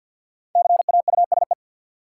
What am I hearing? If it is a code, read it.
Morse code, heard as a keyed tone, about 47 wpm, 700 Hz. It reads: CAUSE
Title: Sentences from Top 200 Words - New Word Drill - 50wpm